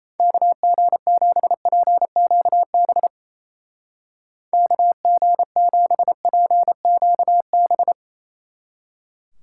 Morse code, heard as a keyed tone, about 36 wpm, 700 Hz. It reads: KG7PQB KG7PQB